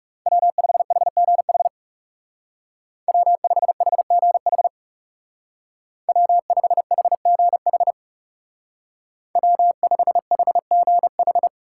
Morse code, heard as a keyed tone, about 36 wpm, 700 Hz. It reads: W5HGH W5HGH W5HGH W5HGH
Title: Speed Racing - USA 1x3 Call Signs 30wpm